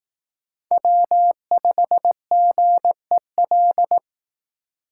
Morse code, heard as a keyed tone, about 18 wpm, 700 Hz. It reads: W5GEL
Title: Speed Racing - USA 1x3 Call Signs 15wpm